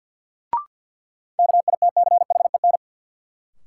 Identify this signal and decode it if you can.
Morse code, heard as a keyed tone, about 45 wpm, 700 Hz. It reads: KITCHEN
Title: Sentences from Top 1400 Words - New Word Drill Alt - 50wpm